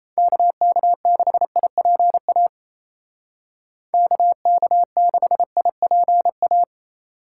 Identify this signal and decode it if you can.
Morse code, heard as a keyed tone, about 33 wpm, 700 Hz. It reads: KK6IPA KK6IPA
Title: Speed Racing - USA 2x3 Call Signs 28wpm